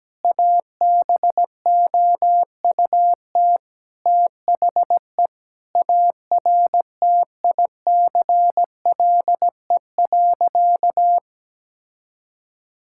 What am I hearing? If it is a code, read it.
Morse code, heard as a keyed tone, about 17 wpm, 700 Hz. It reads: ABOUT THE ARTICLE.
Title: Sentences from Top 1000 Words - Review Part 5 - 17wpm